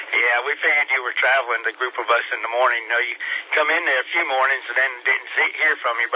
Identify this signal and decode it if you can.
Emergency Alert System (SAME) tones (header bursts), heard as harsh buzzing bursts, with speech. It is heard from 0.5 to 5.7 s.